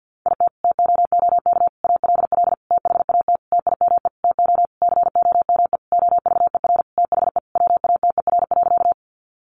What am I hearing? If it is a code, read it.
Morse code, heard as a keyed tone, about 50 wpm, 700 Hz. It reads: IT TOOK ALL THAT TIME TO COME OVER THE WATER.
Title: Sentences from Top 300 Words 50wpm